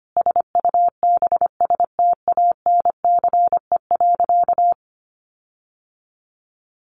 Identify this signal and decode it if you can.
Morse code, heard as a keyed tone, about 25 wpm, 700 Hz. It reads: SUBSTANCE.
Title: Sentences from Top 1500 Words 25wpm